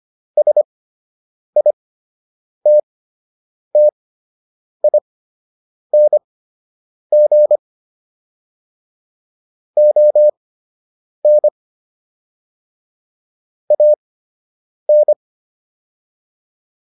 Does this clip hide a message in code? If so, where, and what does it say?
Yes: Morse code from 0.4 to 15.1 s, heard as a keyed tone, about 25 wpm, spaced as for 8 wpm, 600 Hz. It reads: SITTING ON AN